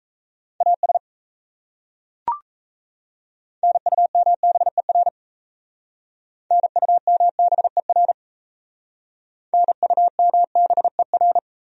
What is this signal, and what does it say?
Morse code, heard as a keyed tone, about 38 wpm, 700 Hz. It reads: AS NUMBER NUMBER NUMBER